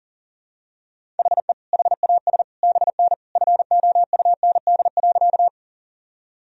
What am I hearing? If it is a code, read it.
Morse code, heard as a keyed tone, about 40 wpm, 700 Hz. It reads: HE HAS BN FOUND.
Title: Sentences from Top 300 Words and Abbreviations 40wpm